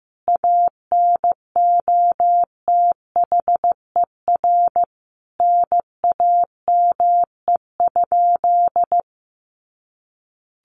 Morse code, heard as a keyed tone, about 15 wpm, 700 Hz. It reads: ANOTHER NAME?
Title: Sentences from Top 600 Words 15wpm